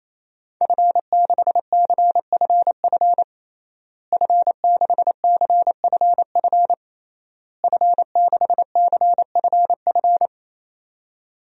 Morse code, heard as a keyed tone, about 28 wpm, 700 Hz. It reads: F6CFF F6CFF F6CFF
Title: DXCC-100 Call Signs - 5 Characters - Repeated 3x - 28wpm